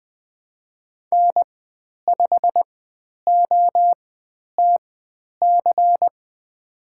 Morse code, heard as a keyed tone, about 20 wpm, spaced as for 10 wpm, 700 Hz. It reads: N5OTC
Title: 109 Lesson 18g - Letter m - Call Signs 20wpm